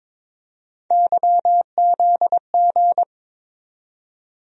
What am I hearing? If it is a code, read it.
Morse code, heard as a keyed tone, about 22 wpm, 700 Hz. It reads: YZG
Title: USA Call Sign Postfixes - 3 Letters 22wpm